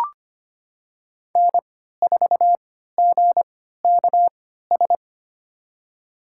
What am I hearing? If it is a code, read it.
Morse code, heard as a keyed tone, about 25 wpm, spaced as for 14 wpm, 700 Hz. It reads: N4GKS
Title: USA Call Signs - 5 Characters - 3x Character Spacing - 25wpm